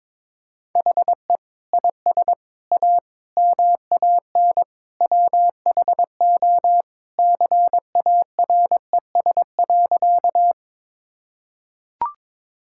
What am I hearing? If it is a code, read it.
Morse code, heard as a keyed tone, about 22 wpm, 700 Hz. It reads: HE IS A MAN WHO CARES.
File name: practice / 22wpm / Sentences with 3-Word Phrases - 22wpm